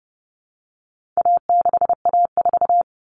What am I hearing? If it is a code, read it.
Morse code, heard as a keyed tone, about 29 wpm, 700 Hz. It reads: A6A4
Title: All Cut Numbers - 4-Digit 30wpm